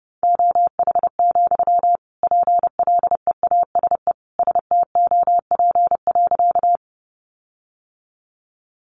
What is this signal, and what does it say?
Morse code, heard as a keyed tone, about 28 wpm, 700 Hz. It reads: OH, PLEASE STOP.
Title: Sentences from Top 1500 Words 30wpm